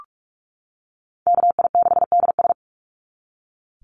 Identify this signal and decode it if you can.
Morse code, heard as a keyed tone, about 46 wpm, 700 Hz. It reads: KI6DS
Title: NA SOTA Chaser Callsigns 45wpm